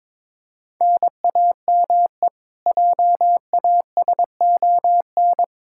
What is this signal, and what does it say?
Morse code, heard as a keyed tone, about 22 wpm, 700 Hz. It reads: NAME JASON
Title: Practice QSO - Exchange 1 - Beginning - 22wpm